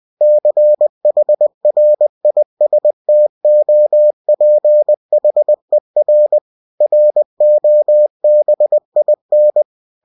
Morse code, heard as a keyed tone, about 20 wpm, 600 Hz. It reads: CHRISTOPHER ROBIN
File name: cw-books / winnie-the-pooh / winnie-20wpm-page-0021